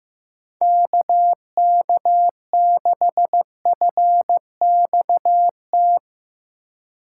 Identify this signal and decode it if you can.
Morse code, heard as a keyed tone, about 16 wpm, 700 Hz. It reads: KK6FXT
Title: Speed Racing - USA Call Signs 15wpm